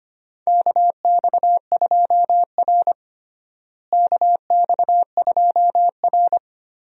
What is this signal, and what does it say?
Morse code, heard as a keyed tone, about 25 wpm, 700 Hz. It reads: KX2R KX2R
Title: USA Call Signs - 4 Characters - Repeated 3x - 25wpm